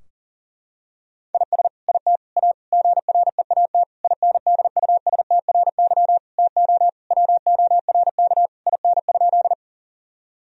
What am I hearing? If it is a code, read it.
Morse code, heard as a keyed tone, about 40 wpm, 700 Hz. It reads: IS IT A GREAT INDUSTRY TO WORK IN?
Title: Sentences from Top 1000 Words - Review Part 4 - 40wpm